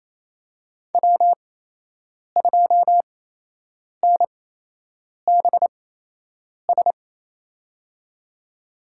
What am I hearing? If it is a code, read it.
Morse code, heard as a keyed tone, about 27 wpm, spaced as for 8 wpm, 700 Hz. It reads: W2NBS
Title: USA Call Signs - 5 Characters - 8x Character Spacing - 28wpm